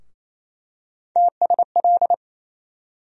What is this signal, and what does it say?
Morse code, heard as a keyed tone, about 28 wpm, 700 Hz. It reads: TSL
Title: USA Call Sign Postfixes - 3 Letters 28wpm